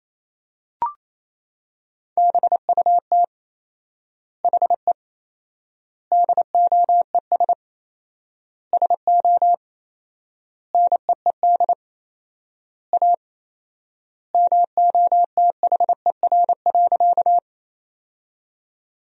Morse code, heard as a keyed tone, about 28 wpm, 700 Hz. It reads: BUT HE DOES SO NEED A MOTHER.
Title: 4x Word-Spacing - Sentences from Top 200 Words - 28wpm